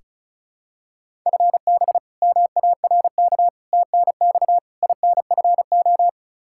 Morse code, heard as a keyed tone, about 36 wpm, 700 Hz. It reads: FB MARK TNX INFO
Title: K1USN SST QSO Practice 35wpm